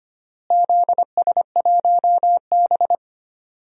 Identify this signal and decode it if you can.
Morse code, heard as a keyed tone, about 25 wpm, 700 Hz. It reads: ZS1B
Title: DXCC-100 Call Signs - 4 Characters - Repeated 3x - 25wpm